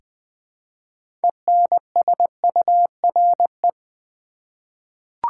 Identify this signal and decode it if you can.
Morse code, heard as a keyed tone, about 20 wpm, 700 Hz. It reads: ENSURE
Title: Sentences from Top 1100 Words - New Word Drill Alt v2 - 20wpm